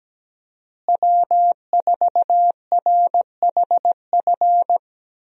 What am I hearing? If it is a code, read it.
Morse code, heard as a keyed tone, about 17 wpm, 700 Hz. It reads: W4RHF
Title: USA Call Signs - 5 Characters - Repeated 3x - 17wpm